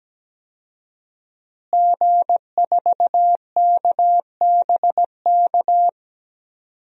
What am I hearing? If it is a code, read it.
Morse code, heard as a keyed tone, about 17 wpm, 700 Hz. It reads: G4KBK
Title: DXCC-100 Call Signs - 5 Characters - Repeated 2x - 17wpm